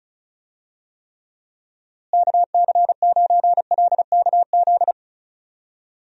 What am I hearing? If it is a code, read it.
Morse code, heard as a keyed tone, about 34 wpm, 700 Hz. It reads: KC9LKZ
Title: USA Call Signs - 6 Characters - Repeated 2x - 35wpm